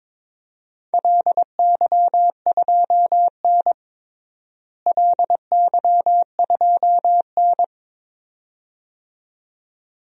Morse code, heard as a keyed tone, about 22 wpm, 700 Hz. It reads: LY2N LY2N
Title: DXCC-100 Call Signs - 4 Characters - Repeated 2x - 22wpm